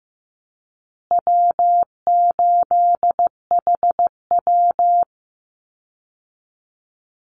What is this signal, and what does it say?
Morse code, heard as a keyed tone, about 15 wpm, 700 Hz. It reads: W8HW
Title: Easy Call Signs Repeated 2x - 15wpm